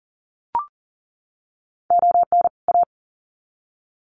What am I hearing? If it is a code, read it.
Morse code, heard as a keyed tone, about 40 wpm, 700 Hz. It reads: ON A
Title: Sets of 2 Words 40wpm